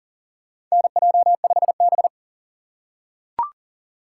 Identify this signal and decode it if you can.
Morse code, heard as a keyed tone, about 40 wpm, 700 Hz. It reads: NJ5B
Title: USA Call Signs - 6x Character Spacing - 40wpm